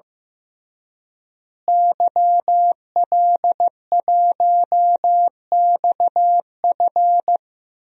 Morse code, heard as a keyed tone, about 16 wpm, 700 Hz. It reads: YL1XF
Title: DXCC-100 Call Signs - 5 Characters - Repeated 3x - 15wpm